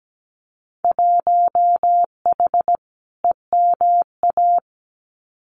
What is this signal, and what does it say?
Morse code, heard as a keyed tone, about 17 wpm, 700 Hz. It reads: 1H EMA
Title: Winter Field Day Exchanges 17wpm